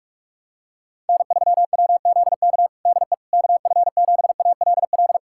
Morse code, heard as a keyed tone, about 46 wpm, 700 Hz. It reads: N3WZK DE KF7ALL
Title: USA Callsign DE Callsign 45wpm